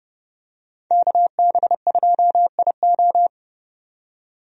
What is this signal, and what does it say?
Morse code, heard as a keyed tone, about 30 wpm, 700 Hz. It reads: KB2IO
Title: USA Call Signs - 5 Characters - 30wpm